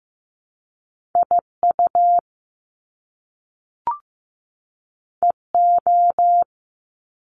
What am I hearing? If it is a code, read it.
Morse code, heard as a keyed tone, about 15 wpm, 700 Hz. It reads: IU EO
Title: Top Two Letter Combinations in Words 15wpm